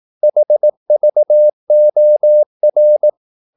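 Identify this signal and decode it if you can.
Morse code, heard as a keyed tone, about 18 wpm, 600 Hz. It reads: HVOR
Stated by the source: Slow (QRS) Morse practice at 18 wpm, CW med Gnister.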